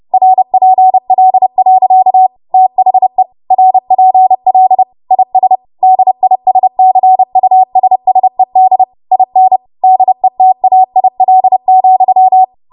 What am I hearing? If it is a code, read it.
Morse code, heard as a keyed tone, about 30 wpm, 750 Hz. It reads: RPL. THE RPL IS DISCUSSED IN DETAIL,
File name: Morse / 30WPM